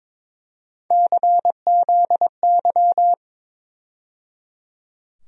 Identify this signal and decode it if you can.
Morse code, heard as a keyed tone, about 21 wpm, 700 Hz. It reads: CZY
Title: USA Call Sign Postfixes - 3 Letters 22wpm